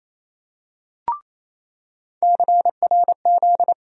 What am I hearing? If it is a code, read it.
Morse code, heard as a keyed tone, about 28 wpm, 700 Hz. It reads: CRZ
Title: USA Call Sign Postfixes - 3 Letters 28wpm